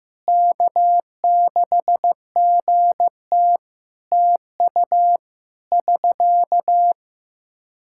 Morse code, heard as a keyed tone, about 15 wpm, 700 Hz. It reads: K6GT TU <SK>
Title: Practice QSO - Exchange 4 - Ending - 15wpm